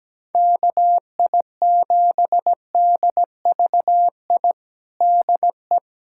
Morse code, heard as a keyed tone, about 17 wpm, 700 Hz. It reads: KI7DVI DE
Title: USA Callsign DE Callsign - 6 Characters - 17wpm